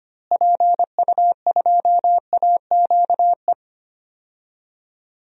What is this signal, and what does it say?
Morse code, heard as a keyed tone, about 25 wpm, 700 Hz. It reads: PU2AQE
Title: DXCC-100 Call Signs - 6 Characters - Repeated 2x - 25wpm